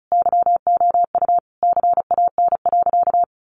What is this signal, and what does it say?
Morse code, heard as a keyed tone, about 34 wpm, 700 Hz. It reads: YOU CAN.
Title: Sentences from Top 500 Words 35wpm